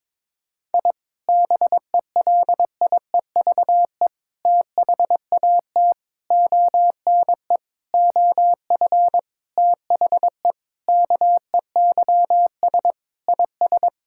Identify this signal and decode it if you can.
Morse code, heard as a keyed tone, about 22 wpm, 700 Hz. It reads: I BELIEVE THAT ONE OF THE KEYS IS
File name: practice / 22wpm / Sentences with 4-Word Phrases - 22wpm